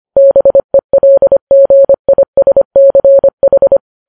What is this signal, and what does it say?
Morse code, heard as a keyed tone, about 25 wpm, 550 Hz. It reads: BELGISCH
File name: Files Web tls / Morse on9fv / morse25kps